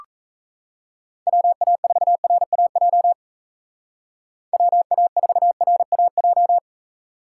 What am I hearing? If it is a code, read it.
Morse code, heard as a keyed tone, about 33 wpm, 700 Hz. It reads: WA4RAJ WA4RAJ
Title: Speed Racing - USA Call Signs 28wpm